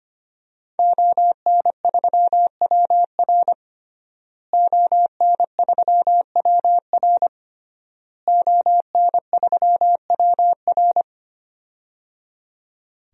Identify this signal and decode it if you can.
Morse code, heard as a keyed tone, about 25 wpm, 700 Hz. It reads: ON3WR ON3WR ON3WR
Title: DXCC-100 Call Signs - 5 Characters - Repeated 3x - 25wpm